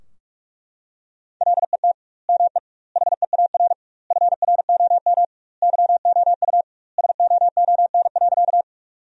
Morse code, heard as a keyed tone, about 41 wpm, 700 Hz. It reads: LET ME HEAR FROM YOU SOON.
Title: 2x Word-Spacing - Sentences from Top 400 Words - 45wpm